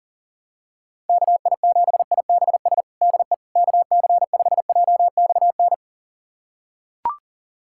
Morse code, heard as a keyed tone, about 40 wpm, 700 Hz. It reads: KI7IBS DE KC5JXN
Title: USA Callsign DE Callsign 40wpm